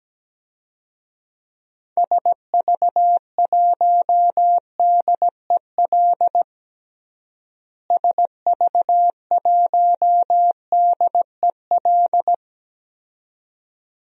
Morse code, heard as a keyed tone, about 17 wpm, 700 Hz. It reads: SV1DEL SV1DEL
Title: DXCC-100 Call Signs - 6 Characters - Repeated 2x - 17wpm